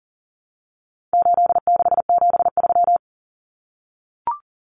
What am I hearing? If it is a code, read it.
Morse code, heard as a keyed tone, about 40 wpm, 700 Hz. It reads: 8673
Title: Common Cut Numbers - 4-Digit 40wpm